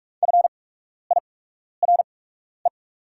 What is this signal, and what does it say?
Morse code, heard as a keyed tone, about 44 wpm, spaced as for 12 wpm, 690 Hz. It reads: FIRE
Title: 8x Character Spacing - Top 500 Words - 45wpm